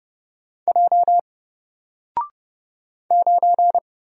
Morse code, heard as a keyed tone, about 31 wpm, 700 Hz. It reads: J 9
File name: Single-Letter-Number-30wpm